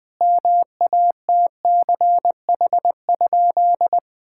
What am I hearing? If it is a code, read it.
Morse code, heard as a keyed tone, about 20 wpm, 700 Hz. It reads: MATCH?
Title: Sentences from Top 1600 Words 20wpm